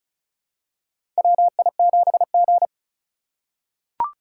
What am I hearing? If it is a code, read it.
Morse code, heard as a keyed tone, about 36 wpm, 700 Hz. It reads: WI7G
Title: USA Call Signs - 4 Characters - 4x Character Spacing - 35wpm